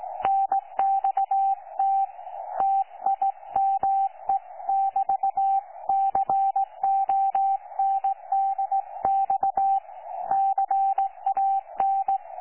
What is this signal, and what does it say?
Morse code, heard as a keyed tone, about 17 wpm, 780 Hz. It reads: NXT TIME<BT>CONDX CAN
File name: 12 39Z 18088.2kHz utah aa7fv k9tjl